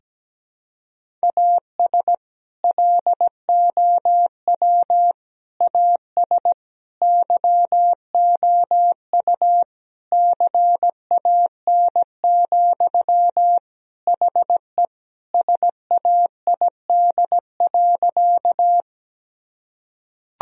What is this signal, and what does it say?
Morse code, heard as a keyed tone, about 17 wpm, 700 Hz. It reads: AS LOW AS YOU CAN, HE SAID.
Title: Sentences from Top 1000 Words - Review Part 1 - 17wpm